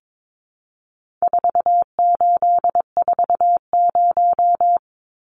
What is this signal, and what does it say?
Morse code, heard as a keyed tone, about 22 wpm, 700 Hz. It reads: 4840
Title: Numbers - 4-Digit 22wpm